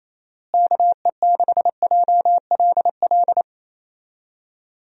Morse code, heard as a keyed tone, about 28 wpm, 700 Hz. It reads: KE6JLL
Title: USA Call Signs - 6 Characters - Repeated 3x - 28wpm